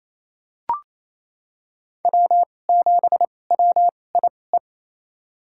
Morse code, heard as a keyed tone, about 27 wpm, spaced as for 20 wpm, 700 Hz. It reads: W7WIE
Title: USA Call Signs - 2x Character Spacing - 28wpm